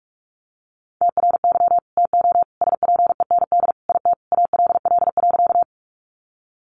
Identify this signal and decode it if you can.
Morse code, heard as a keyed tone, about 44 wpm, 700 Hz. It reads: TRY TO SPEND IT ALL.
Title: Sentences from Top 800 Words 45wpm